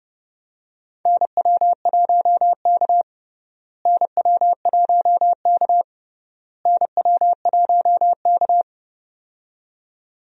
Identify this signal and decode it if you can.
Morse code, heard as a keyed tone, about 30 wpm, 700 Hz. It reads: NW1K NW1K NW1K
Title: USA Call Signs - 4 Characters - Repeated 3x - 30wpm